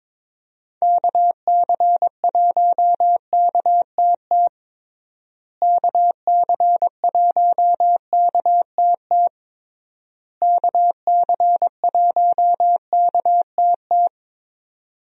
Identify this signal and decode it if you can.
Morse code, heard as a keyed tone, about 22 wpm, 700 Hz. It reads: KC1KTT KC1KTT KC1KTT
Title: USA Call Signs - 6 Characters - Repeated 3x - 22wpm